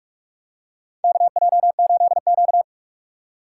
Morse code, heard as a keyed tone, about 46 wpm, 700 Hz. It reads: KJ8Q
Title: USA Call Signs - 3x Character Spacing - 45wpm